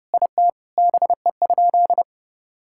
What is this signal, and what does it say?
Morse code, heard as a keyed tone, about 32 wpm, 700 Hz. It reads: IT BE?
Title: Sentences from Top 1000 Words - Review Part 5 - 30wpm